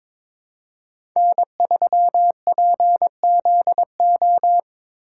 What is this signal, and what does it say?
Morse code, heard as a keyed tone, about 26 wpm, 700 Hz. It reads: N3PZO
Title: Speed Racing - USA Call Signs 22wpm